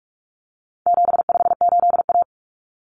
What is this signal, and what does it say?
Morse code, heard as a keyed tone, about 44 wpm, 700 Hz. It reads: 758A
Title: Common Cut Numbers - 4-Digit 45wpm